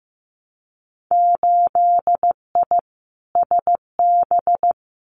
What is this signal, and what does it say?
Morse code, heard as a keyed tone, about 15 wpm, 700 Hz. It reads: 8I SB
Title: Winter Field Day Exchanges Encore 15wpm